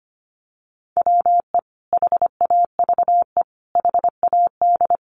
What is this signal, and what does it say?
Morse code, heard as a keyed tone, about 25 wpm, 700 Hz. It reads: WE HAVE HAD
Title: Sentences from Top 700 Words 25wpm